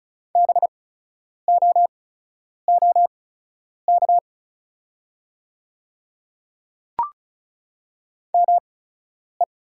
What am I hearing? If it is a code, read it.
Morse code, heard as a keyed tone, about 36 wpm, spaced as for 10 wpm, 690 Hz. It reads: BOOK ME
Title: 8x Character Spacing - Top 500 Words - 35wpm